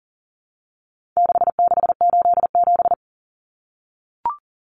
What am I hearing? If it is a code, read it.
Morse code, heard as a keyed tone, about 40 wpm, 700 Hz. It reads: 6687
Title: Common Cut Numbers - 4-Digit 40wpm